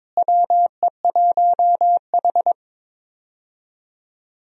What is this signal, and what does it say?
Morse code, heard as a keyed tone, about 22 wpm, 700 Hz. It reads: WE1H
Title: USA Call Signs - 4 Characters - Repeated 3x - 22wpm